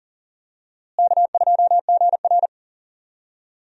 Morse code, heard as a keyed tone, about 39 wpm, 700 Hz. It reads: K2GR
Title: USA Call Signs - 4 Characters - Repeated 2x - 40wpm